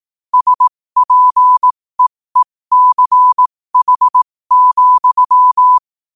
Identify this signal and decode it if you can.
Morse code, heard as a keyed tone, about 18 wpm, spaced as for 15 wpm, 1000 Hz. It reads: SPEECH,
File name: qotdfiles / 15wpm